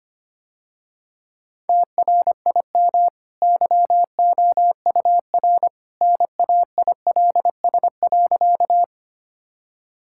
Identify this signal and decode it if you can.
Morse code, heard as a keyed tone, about 25 wpm, 700 Hz. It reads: TRIM YOUR NAILS.
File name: Easy Sentences with Hints - 25wpm